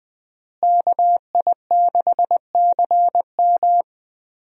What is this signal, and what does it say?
Morse code, heard as a keyed tone, about 20 wpm, 700 Hz. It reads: KI6CM
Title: USA Call Signs - 5 Characters - Repeated 3x - 20wpm